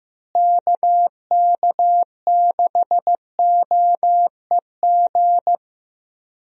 Morse code, heard as a keyed tone, about 15 wpm, 700 Hz. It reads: KK6OEG
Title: USA Call Signs - 6 Characters - Repeated 3x - 15wpm